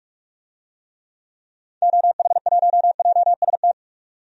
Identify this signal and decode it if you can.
Morse code, heard as a keyed tone, about 42 wpm, 700 Hz. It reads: OH1JST
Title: DXCC-100 Call Signs - 6 Characters - 45wpm